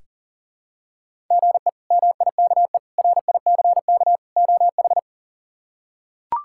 Morse code, heard as a keyed tone, about 37 wpm, 700 Hz. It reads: GE MIKE RICK OH
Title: K1USN SST QSO Practice - 40wpm